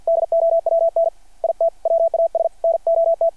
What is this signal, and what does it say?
Morse code, heard as a keyed tone, about 52 wpm, 660 Hz. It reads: DOWN IT WAS NOT
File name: archived / CRSnet